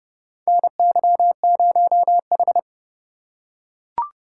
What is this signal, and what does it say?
Morse code, heard as a keyed tone, about 29 wpm, 700 Hz. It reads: NY0H
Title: USA Call Signs - 4 Characters - 3x Character Spacing - 30wpm